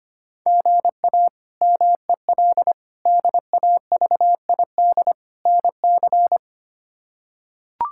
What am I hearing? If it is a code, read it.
Morse code, heard as a keyed tone, about 25 wpm, 700 Hz. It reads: GA MEL DAVID NC
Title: K1USN SST QSO Practice - 25wpm